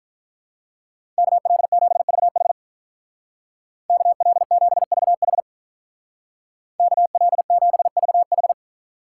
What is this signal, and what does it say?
Morse code, heard as a keyed tone, about 43 wpm, 700 Hz. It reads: KL7VH KL7VH KL7VH
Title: Speed Racing - USA 2x2 Call Signs 35wpm